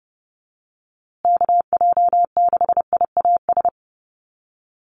Morse code, heard as a keyed tone, about 30 wpm, 700 Hz. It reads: KJ6IAS
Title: USA Call Signs Repeated 2x 30wpm